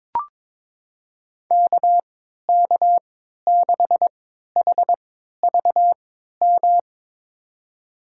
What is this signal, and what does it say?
Morse code, heard as a keyed tone, about 22 wpm, spaced as for 12 wpm, 700 Hz. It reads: KK6HVM
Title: USA Call Signs - 6 Characters - 3x Character Spacing - 22wpm